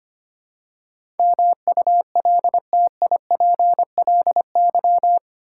Morse code, heard as a keyed tone, about 25 wpm, 700 Hz. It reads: MULTIPLY
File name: Eight Letter Words - 25wpm